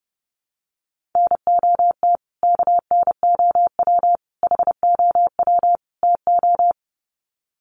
Morse code, heard as a keyed tone, about 30 wpm, 700 Hz. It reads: NOT KNOW HOW TO
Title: Sets of 4 Words 30wpm